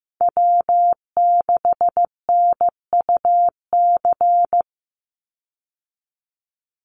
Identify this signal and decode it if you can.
Morse code, heard as a keyed tone, about 15 wpm, 700 Hz. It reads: W6NUC
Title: Call Signs Repeated 3x - 15wpm